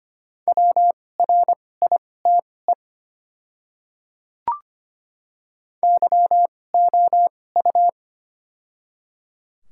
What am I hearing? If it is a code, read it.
Morse code, heard as a keyed tone, about 25 wpm, spaced as for 18 wpm, 700 Hz. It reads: WRITE YOU
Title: 2x Character Spacing - Top 100 Words - 25wpm